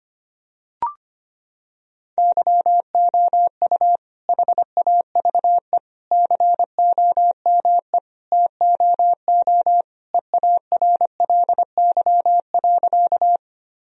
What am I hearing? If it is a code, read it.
Morse code, heard as a keyed tone, about 25 wpm, 700 Hz. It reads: YOU HAVE COME TOO EARLY.
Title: Sentences from Top 1000 Words - Review Part 2 - 25wpm